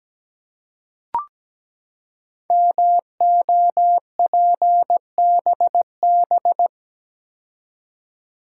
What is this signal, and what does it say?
Morse code, heard as a keyed tone, about 17 wpm, 700 Hz. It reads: MOPBB
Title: Code Groups - 5 Letters - 17wpm